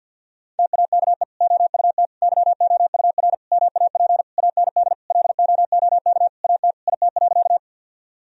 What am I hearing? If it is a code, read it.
Morse code, heard as a keyed tone, about 51 wpm, 700 Hz. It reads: TAKE OUT YOUR MAP AND LOOK AT IT.
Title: Sentences from Top 1000 Words - Review Part 2 - 50wpm